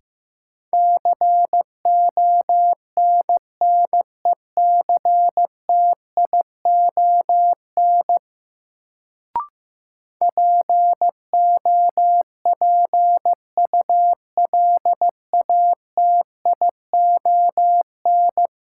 Morse code, heard as a keyed tone, about 15 wpm, 700 Hz. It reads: CONNECTION POPULATION
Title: Sentences from Top 1300 Words - New Word Drill Alt - 15wpm